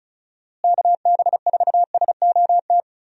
Morse code, heard as a keyed tone, about 36 wpm, 700 Hz. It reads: KB4SOT
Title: USA Call Signs - 6 Characters - Repeated 2x - 35wpm